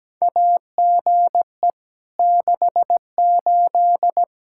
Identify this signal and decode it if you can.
Morse code, heard as a keyed tone, about 17 wpm, 700 Hz. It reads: AGE 68
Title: Practice QSO - Exchange 3 - More Info - 17wpm